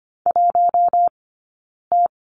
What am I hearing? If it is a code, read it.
Morse code, heard as a keyed tone, about 25 wpm, spaced as for 9 wpm, 690 Hz. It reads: 1T
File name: Lesson 15d - Number 5 - Two Characters 25wpm